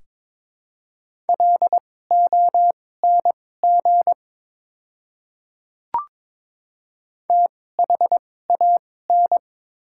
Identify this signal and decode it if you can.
Morse code, heard as a keyed tone, about 22 wpm, spaced as for 16 wpm, 690 Hz. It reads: LONG THAN
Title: 2x Character Spacing - Top 100 Words - 22wpm